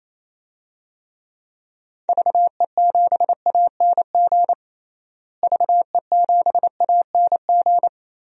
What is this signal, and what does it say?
Morse code, heard as a keyed tone, about 28 wpm, 700 Hz. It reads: VE7ANG VE7ANG
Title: DXCC-100 Call Signs - 6 Characters - Repeated 3x - 28wpm